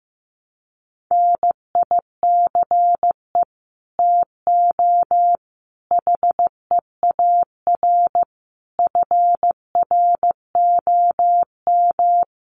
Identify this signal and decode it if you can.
Morse code, heard as a keyed tone, about 15 wpm, 700 Hz. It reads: NICE TO HEAR FROM
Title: Common QSO Phrases 15wpm